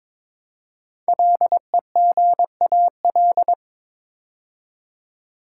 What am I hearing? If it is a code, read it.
Morse code, heard as a keyed tone, about 22 wpm, 700 Hz. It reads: LEGAL
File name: 5 Letter Words - First 3 Letters Given - 22wpm